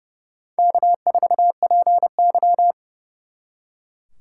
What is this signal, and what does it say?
Morse code, heard as a keyed tone, about 39 wpm, 700 Hz. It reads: K4PY